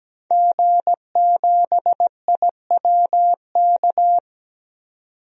Morse code, heard as a keyed tone, about 17 wpm, 700 Hz. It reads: G7IWK
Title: DXCC-100 Call Signs - 5 Characters - 17wpm